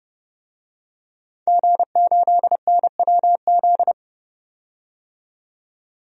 Morse code, heard as a keyed tone, about 30 wpm, 700 Hz. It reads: G8NWZ
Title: DXCC-100 Call Signs - 5 Characters - 30wpm